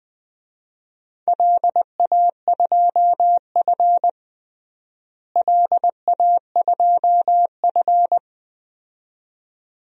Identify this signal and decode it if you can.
Morse code, heard as a keyed tone, about 21 wpm, 700 Hz. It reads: LA2F LA2F
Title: DXCC-100 Call Signs - 4 Characters - Repeated 2x - 20wpm